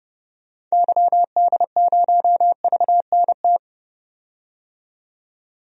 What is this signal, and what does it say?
Morse code, heard as a keyed tone, about 32 wpm, 700 Hz. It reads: YD0VNT
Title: DXCC-100 Call Signs - 6 Characters - 30wpm